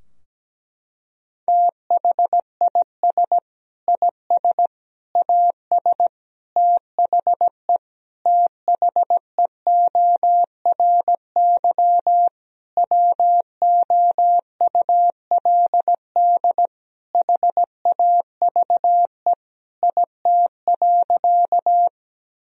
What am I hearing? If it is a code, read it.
Morse code, heard as a keyed tone, about 17 wpm, 700 Hz. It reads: THIS IS AS THE THEORY WOULD HAVE IT.
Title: Sentences from Top 1000 Words - Review Part 3 - 17wpm